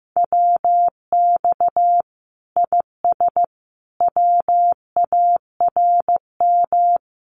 Morse code, heard as a keyed tone, about 15 wpm, 700 Hz. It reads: WX IS WARM
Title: Common QSO Phrases 15wpm